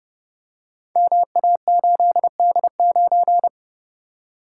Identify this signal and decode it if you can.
Morse code, heard as a keyed tone, about 30 wpm, 700 Hz. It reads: MA8D9
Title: Code Groups - 5 Letter:Numbers - 30wpm